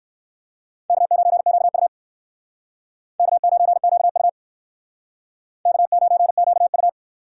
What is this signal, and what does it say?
Morse code, heard as a keyed tone, about 68 wpm, 700 Hz. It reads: K9QU K9QU K9QU